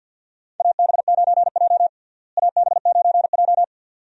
Morse code, heard as a keyed tone, about 51 wpm, 700 Hz. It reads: AB9J AB9J
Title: USA Call Signs - 4 Characters - Repeated 2x - 50wpm